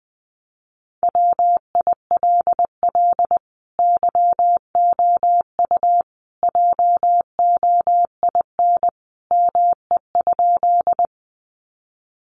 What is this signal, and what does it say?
Morse code, heard as a keyed tone, about 19 wpm, 700 Hz. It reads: WILL YOU JOIN ME?
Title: Sentences from Top 1400 Words 20wpm